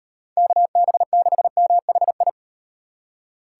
Morse code, heard as a keyed tone, about 35 wpm, 700 Hz. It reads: KB6MHI